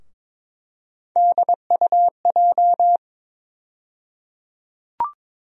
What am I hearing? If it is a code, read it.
Morse code, heard as a keyed tone, about 22 wpm, 700 Hz. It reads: DUJ
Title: USA Call Sign Postfixes - 3 Letters 22wpm